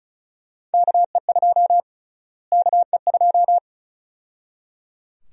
Morse code, heard as a keyed tone, about 34 wpm, 700 Hz. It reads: KE2 KE2
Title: USA Call Sign Prefixes - Repeated 2x - 35wpm